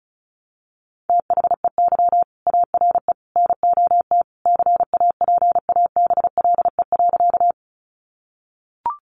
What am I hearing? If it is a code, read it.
Morse code, heard as a keyed tone, about 34 wpm, 700 Hz. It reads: THEY ARE NOT CAPABLE.
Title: Sentences from Top 1200 Words 35wpm